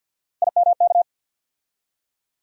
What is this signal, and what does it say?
Morse code, heard as a keyed tone, about 50 wpm, 700 Hz. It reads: IMK
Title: USA Call Sign Postfixes - 3 Letters 50wpm